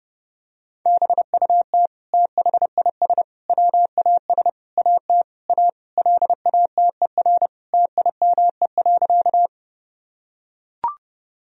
Morse code, heard as a keyed tone, about 32 wpm, 700 Hz. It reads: BUT THIS WAS AT A LATER TIME.
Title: Sentences from Top 1000 Words - Review Part 4 - 30wpm